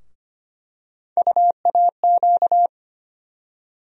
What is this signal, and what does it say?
Morse code, heard as a keyed tone, about 25 wpm, 700 Hz. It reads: UAQ
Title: USA Call Sign Postfixes - 3 Letters 25wpm